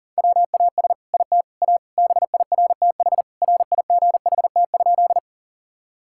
Morse code, heard as a keyed tone, about 40 wpm, 700 Hz. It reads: WAS IT A BIRTH RIGHT?
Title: Sentences from Top 1000 Words - Review Part 5 - 40wpm